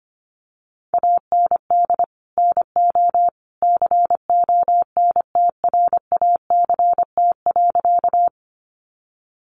Morse code, heard as a keyed tone, about 25 wpm, 700 Hz. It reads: AND NO CONTRACT.
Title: Sentences from Top 1400 Words 25wpm